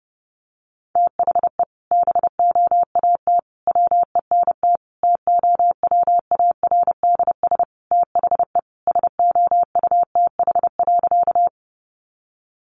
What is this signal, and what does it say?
Morse code, heard as a keyed tone, about 28 wpm, 700 Hz. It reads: THE BOAT WENT TOWARDS THE SOUTH.
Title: Sentences from Top 1100 Words 30wpm